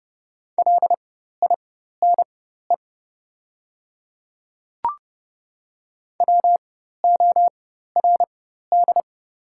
Morse code, heard as a keyed tone, about 30 wpm, spaced as for 14 wpm, 700 Hz. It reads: LINE WORD
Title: 4x Character Spacing - Top 300 Words - 30wpm